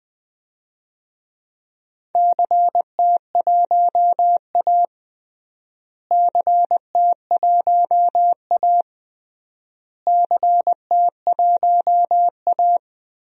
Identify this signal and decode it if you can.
Morse code, heard as a keyed tone, about 20 wpm, 700 Hz. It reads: CT1A CT1A CT1A
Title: DXCC-100 Call Signs - 4 Characters - Repeated 3x - 20wpm